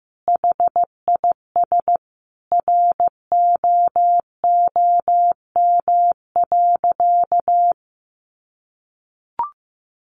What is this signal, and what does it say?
Morse code, heard as a keyed tone, about 15 wpm, 700 Hz. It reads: HIS ROOM.
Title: Sentences from Top 300 Words 15wpm